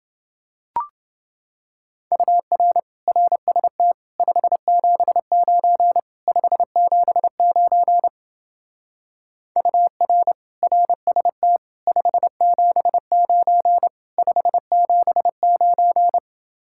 Morse code, heard as a keyed tone, about 28 wpm, 700 Hz. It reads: UR RST 579 579 UR RST 579 579